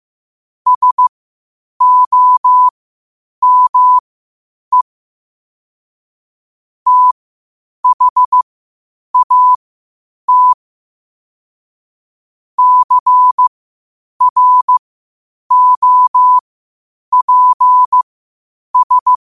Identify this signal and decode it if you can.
Morse code, heard as a keyed tone, about 15 wpm, spaced as for 8 wpm, 1000 Hz. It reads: SOME THAT CROPS
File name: qotdfiles / 8wpm